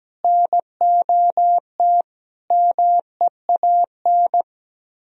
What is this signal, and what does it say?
Morse code, heard as a keyed tone, about 17 wpm, 700 Hz. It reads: NOT MEAN
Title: Speed-Racing - Sets of 5 Words 17wpm